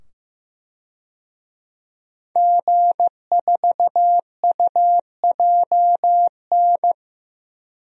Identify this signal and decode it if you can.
Morse code, heard as a keyed tone, about 15 wpm, 700 Hz. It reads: G4UJN
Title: DXCC-100 Call Signs - 5 Characters - 15wpm